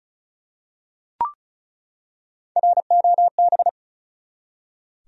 Morse code, heard as a keyed tone, about 36 wpm, 700 Hz. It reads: ROB
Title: USA Common Names - 35wpm